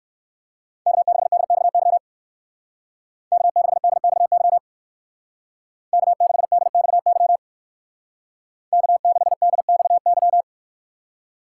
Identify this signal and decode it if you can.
Morse code, heard as a keyed tone, about 64 wpm, 700 Hz. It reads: K6DXY K6DXY K6DXY K6DXY